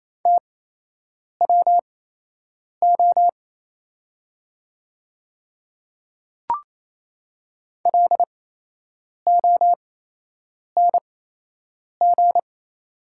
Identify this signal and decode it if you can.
Morse code, heard as a keyed tone, about 28 wpm, spaced as for 8 wpm, 700 Hz. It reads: TWO LONG
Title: 8x Character Spacing - Top 100 Words - 28wpm